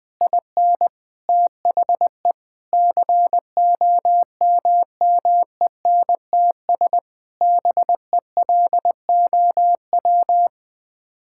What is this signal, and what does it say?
Morse code, heard as a keyed tone, about 20 wpm, 700 Hz. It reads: IN THE COMMENTS BELOW
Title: Sets of 9 Words - 20wpm